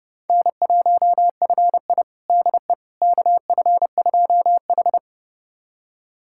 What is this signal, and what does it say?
Morse code, heard as a keyed tone, about 32 wpm, 700 Hz. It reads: N1FI DE KF2H
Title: USA Callsign DE Callsign - 4 Characters - 30wpm